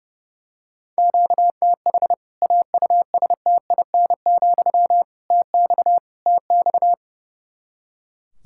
Morse code, heard as a keyed tone, about 30 wpm, 700 Hz. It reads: QTH AUSTIN, TX TX
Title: K1USN SST QSO Practice 30wpm